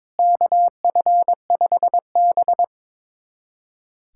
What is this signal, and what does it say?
Morse code, heard as a keyed tone, about 23 wpm, 700 Hz. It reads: KF5B
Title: USA Callsign DE Callsign - 4 Characters - 22wpm